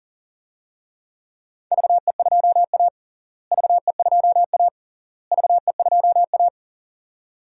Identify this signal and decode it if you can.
Morse code, heard as a keyed tone, about 40 wpm, 700 Hz. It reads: VE2A VE2A VE2A
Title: DXCC-100 Call Signs - 4 Characters - Repeated 3x - 40wpm